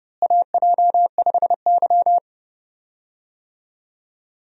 Morse code, heard as a keyed tone, about 32 wpm, 700 Hz. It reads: AJ5Y
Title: USA Call Signs - 4 Characters - Repeated 3x - 30wpm